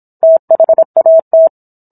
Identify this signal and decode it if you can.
Morse code, heard as a keyed tone, about 26 wpm, 650 Hz. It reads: THAT